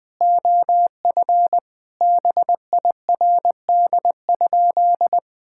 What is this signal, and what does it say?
Morse code, heard as a keyed tone, about 20 wpm, 700 Hz. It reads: OF BIRD?
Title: Sentences with 8-Word Phrases - Spoken Beginning - 20wpm